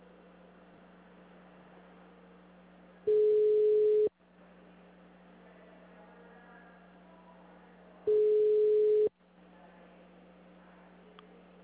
Ringback tone (Europe, 425 Hz).
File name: out- -2007- - - .79